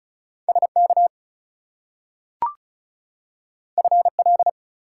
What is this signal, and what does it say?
Morse code, heard as a keyed tone, about 36 wpm, 700 Hz. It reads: SK FL
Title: US States and Canadian Provinces 35wpm